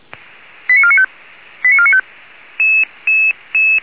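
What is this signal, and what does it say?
Five-tone selective call (ZVEI), heard as quick runs of five tones, tones of about 70 ms. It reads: 89495 89495
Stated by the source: ZVEI five-tone selective call received by SDR, recorded by Feuerwehr Orlamünde (fire brigade).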